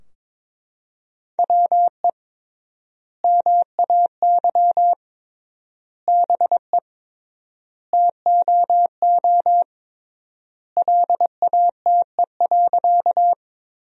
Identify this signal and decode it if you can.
Morse code, heard as a keyed tone, about 22 wpm, 700 Hz. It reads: WE MAY BE TOO LATE.
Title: 3x Word-Spacing - Sentences from Top 300 Words - 22wpm